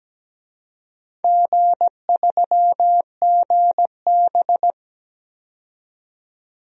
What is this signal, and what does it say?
Morse code, heard as a keyed tone, about 17 wpm, 700 Hz. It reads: G3GB
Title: DXCC-100 Call Signs - 4 Characters - 17wpm